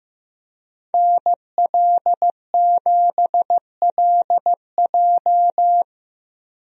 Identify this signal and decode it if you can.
Morse code, heard as a keyed tone, about 15 wpm, 700 Hz. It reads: NL7LJ
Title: USA Call Signs - 5 Characters - Repeated 2x - 15wpm